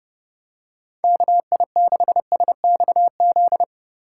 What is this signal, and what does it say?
Morse code, heard as a keyed tone, about 29 wpm, 700 Hz. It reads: KI6SXZ